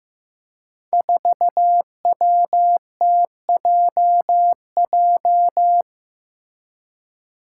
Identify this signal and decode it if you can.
Morse code, heard as a keyed tone, about 15 wpm, 700 Hz. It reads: 4WTJJ
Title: Five Challenging Letter-Number Combinations 15wpm